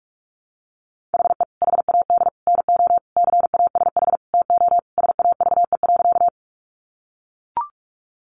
Morse code, heard as a keyed tone, about 44 wpm, 700 Hz. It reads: HE HAD NO CASH TO SAVE.
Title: Sentences from Top 1100 Words 45wpm